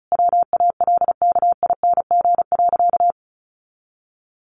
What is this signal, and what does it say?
Morse code, heard as a keyed tone, about 34 wpm, 700 Hz. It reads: WALKING.
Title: Additional Easy Phrases 35wpm